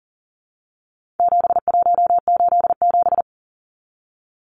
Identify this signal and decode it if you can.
Morse code, heard as a keyed tone, about 40 wpm, 700 Hz. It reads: 7187
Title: Numbers - 4-Digit 40wpm